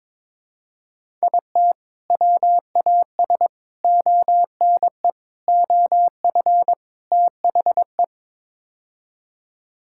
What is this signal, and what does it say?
Morse code, heard as a keyed tone, about 21 wpm, 700 Hz. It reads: IT WAS ONE OF THE
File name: Build A Phrase - 5-Word Phrases - 22wpm